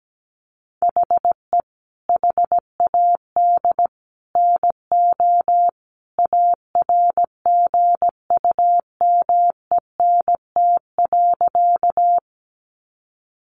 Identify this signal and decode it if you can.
Morse code, heard as a keyed tone, about 17 wpm, 700 Hz. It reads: HE HAD NO ARGUMENT.
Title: Sentences from Top 1400 Words 17wpm